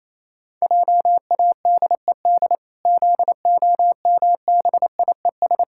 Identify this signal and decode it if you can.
Morse code, heard as a keyed tone, about 28 wpm, 700 Hz. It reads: JADED ZOMBIES
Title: Pangram Challenge v2 - 28wpm